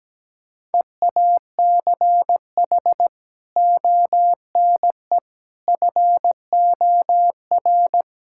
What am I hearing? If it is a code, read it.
Morse code, heard as a keyed tone, about 17 wpm, 700 Hz. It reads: EACH ONE FOR
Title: Sentences from Top 1600 Words 17wpm